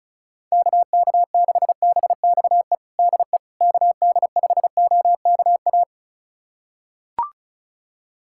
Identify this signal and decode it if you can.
Morse code, heard as a keyed tone, about 34 wpm, 700 Hz. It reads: KK6BXE DE KD5OKA
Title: USA Callsign DE Callsign - 6 Characters - 35wpm